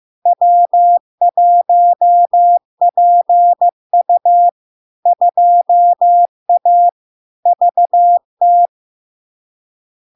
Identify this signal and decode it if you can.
Morse code, heard as a keyed tone, about 15 wpm, 700 Hz. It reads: W1PU 2A VT
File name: LARK-FD-2022-Log-in-CW-15wpm-700-Hz